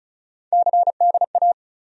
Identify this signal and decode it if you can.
Morse code, heard as a keyed tone, about 36 wpm, 700 Hz. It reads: CDA
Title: USA Call Sign Postfixes - 3 Letters 35wpm